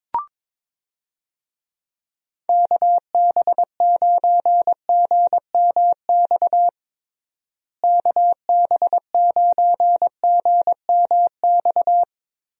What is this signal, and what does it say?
Morse code, heard as a keyed tone, about 22 wpm, 700 Hz. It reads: KB9GMX KB9GMX
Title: USA Call Signs - 6 Characters - Repeated 3x - 22wpm